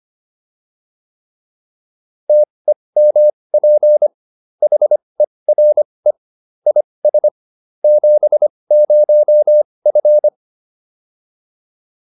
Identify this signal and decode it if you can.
Morse code, heard as a keyed tone, about 25 wpm, spaced as for 20 wpm, 600 Hz. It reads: TEMP HERE IS 70F